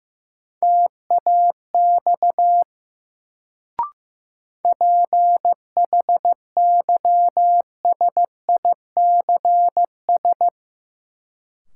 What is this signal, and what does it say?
Morse code, heard as a keyed tone, about 15 wpm, 700 Hz. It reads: TAX PHYSICS
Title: Sentences from Top 1000 Words - Review Part 5 - New Word Drill - 15wpm